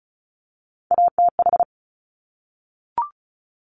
Morse code, heard as a keyed tone, about 36 wpm, 700 Hz. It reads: ATH
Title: Top Three Letter Combinations in Words 35wpm